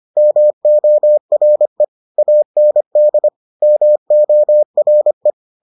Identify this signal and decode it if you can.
Morse code, heard as a keyed tone, about 25 wpm, 600 Hz. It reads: MORE AND MORE